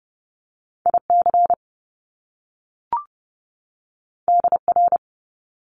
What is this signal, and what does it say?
Morse code, heard as a keyed tone, about 30 wpm, 700 Hz. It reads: IC DR
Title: Top Two Letter Combinations in Words 30wpm